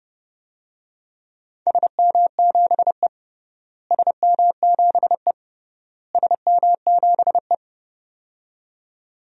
Morse code, heard as a keyed tone, about 30 wpm, 700 Hz. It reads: SM7E SM7E SM7E
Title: DXCC-100 Call Signs - 4 Characters - Repeated 3x - 30wpm